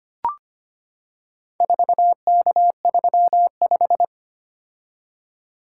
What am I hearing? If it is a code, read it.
Morse code, heard as a keyed tone, about 25 wpm, 700 Hz. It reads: 4K35
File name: practice / 25wpm / Four Letter-Number Combinations - 25wpm